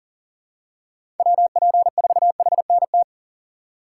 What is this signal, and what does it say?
Morse code, heard as a keyed tone, about 40 wpm, 700 Hz. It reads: WP4HNT
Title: USA Call Signs - 6 Characters - 3x Character Spacing - 40wpm